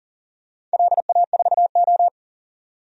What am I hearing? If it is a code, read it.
Morse code, heard as a keyed tone, about 40 wpm, 700 Hz. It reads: LA4O
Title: DXCC-100 Call Signs - 4 Characters - 40wpm